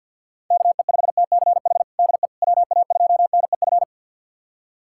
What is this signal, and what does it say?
Morse code, heard as a keyed tone, about 53 wpm, 700 Hz. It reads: KE5TCH DE WA2NEF
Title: USA Callsign DE Callsign 50wpm